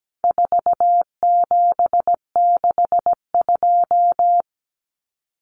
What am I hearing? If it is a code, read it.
Morse code, heard as a keyed tone, about 17 wpm, 700 Hz. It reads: 4762
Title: Numbers - 4-Digit 17wpm